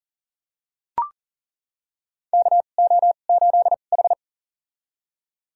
Morse code, heard as a keyed tone, about 40 wpm, spaced as for 29 wpm, 700 Hz. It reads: KO8H
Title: USA Call Signs - 4 Characters - 2x Character Spacing - 40wpm